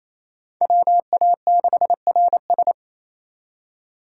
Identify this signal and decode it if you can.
Morse code, heard as a keyed tone, about 35 wpm, 700 Hz. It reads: WA6RS